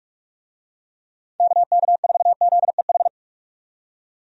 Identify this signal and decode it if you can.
Morse code, heard as a keyed tone, about 42 wpm, 700 Hz. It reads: KK4ZEH
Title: USA Call Signs - 6 Characters - 45wpm